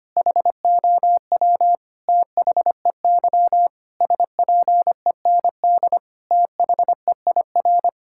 Morse code, heard as a keyed tone, about 25 wpm, 700 Hz. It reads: HOW THEY SPEND THEIR
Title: Sentences from Top 1000 Words - Review Part 5 - 25wpm